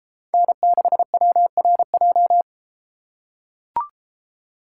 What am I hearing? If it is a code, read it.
Morse code, heard as a keyed tone, about 33 wpm, 700 Hz. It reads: N6WRJ